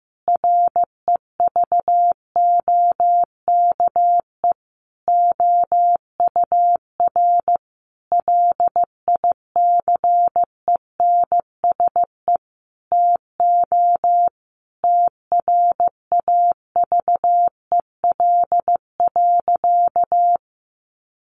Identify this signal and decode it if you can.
Morse code, heard as a keyed tone, about 14 wpm, 700 Hz. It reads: REVOKE OUR LICENSE TO TRAVEL.
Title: A Sound Of Thunder 15wpm